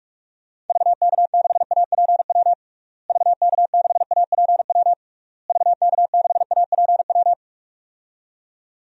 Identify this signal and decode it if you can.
Morse code, heard as a keyed tone, about 41 wpm, 700 Hz. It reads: VK6APW VK6APW VK6APW
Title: DXCC-100 Call Signs - 6 Characters - Repeated 3x - 45wpm